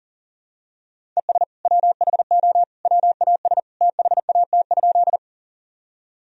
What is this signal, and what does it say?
Morse code, heard as a keyed tone, about 40 wpm, 700 Hz. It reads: ES WHO WAS THAT?
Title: Sentences from Top 100 Words and Abbreviations 40wpm